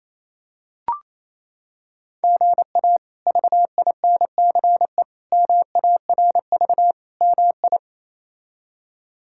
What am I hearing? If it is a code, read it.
Morse code, heard as a keyed tone, about 28 wpm, 700 Hz. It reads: GA VINCE MARV MI
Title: K1USN SST QSO Practice - 28wpm